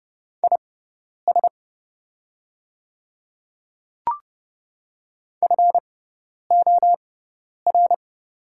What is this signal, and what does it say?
Morse code, heard as a keyed tone, about 30 wpm, spaced as for 10 wpm, 700 Hz. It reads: IS FOR
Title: 6x Character Spacing - Top 200 Words - 30wpm